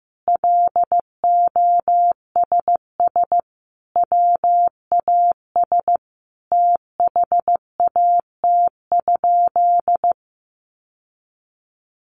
Morse code, heard as a keyed tone, about 15 wpm, 700 Hz. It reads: LOSS WAS THAT?
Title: Sentences from Top 1400 Words 15wpm